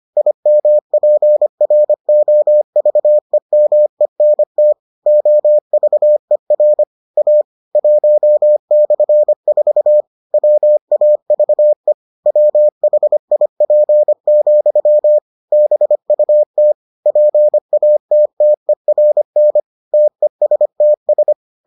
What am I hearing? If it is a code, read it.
Morse code, heard as a keyed tone, about 25 wpm, 600 Hz. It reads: IMPROVEMENT OVER A 1/4 WAVE WHIP, BUT PATTERN TESTS